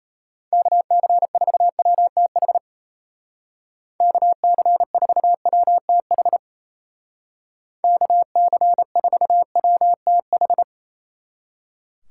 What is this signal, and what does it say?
Morse code, heard as a keyed tone, about 33 wpm, 700 Hz. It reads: KC4WTH KC4WTH KC4WTH
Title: Speed Racing - USA 2x3 Call Signs 28wpm